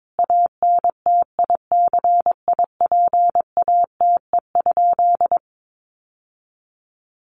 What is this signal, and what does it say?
Morse code, heard as a keyed tone, about 22 wpm, 700 Hz. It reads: ANTICIPATE?
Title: Sentences from Top 1500 Words 22wpm